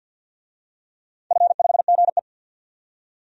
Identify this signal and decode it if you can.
Morse code, heard as a keyed tone, about 46 wpm, 700 Hz. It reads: F5GE
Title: DXCC-100 Call Signs - 4 Characters - 50wpm